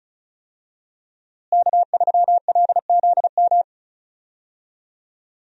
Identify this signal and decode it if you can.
Morse code, heard as a keyed tone, about 36 wpm, 700 Hz. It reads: K3LZM
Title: USA Call Signs - 5 Characters - Repeated 3x - 35wpm